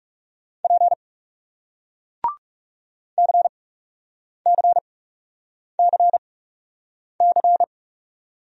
Morse code, heard as a keyed tone, about 41 wpm, spaced as for 8 wpm, 700 Hz. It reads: P CCCC